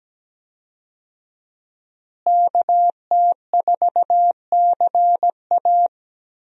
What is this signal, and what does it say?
Morse code, heard as a keyed tone, about 17 wpm, 700 Hz. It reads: KT4CA
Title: USA Call Signs - 5 Characters - Repeated 2x - 17wpm